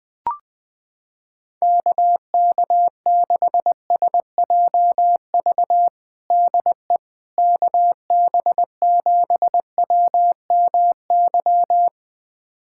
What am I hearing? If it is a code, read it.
Morse code, heard as a keyed tone, about 20 wpm, 700 Hz. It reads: KK6SJV DE KB7WMY
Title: USA Callsign DE Callsign - 6 Characters - 20wpm